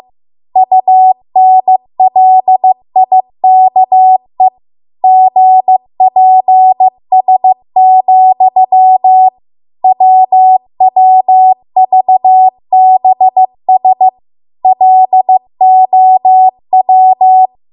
Morse code, heard as a keyed tone, about 15 wpm, 750 Hz. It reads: UNLIKE GPS, WWVBS LOW
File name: Archive / 15 WPM / 15WPM